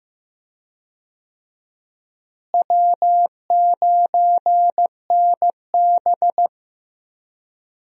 Morse code, heard as a keyed tone, about 15 wpm, 700 Hz. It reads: W9NB
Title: USA Call Signs - 4 Characters - Repeated 2x - 15wpm